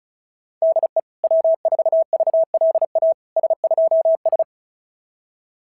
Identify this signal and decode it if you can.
Morse code, heard as a keyed tone, about 36 wpm, 660 Hz. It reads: DE W4VLA S2S
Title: SOTA QSO Practice 35wpm